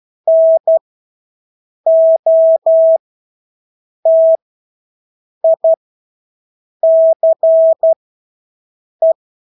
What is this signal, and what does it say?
Morse code, heard as a keyed tone, about 12 wpm, spaced as for 6 wpm, 650 Hz. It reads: NOTICE
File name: Engelske ord 12-6 wpm